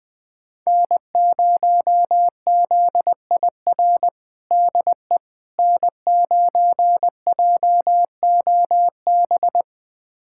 Morse code, heard as a keyed tone, about 20 wpm, 700 Hz. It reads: N0ZIR DE N9JOB
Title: USA Callsign DE Callsign - 5 Characters - 20wpm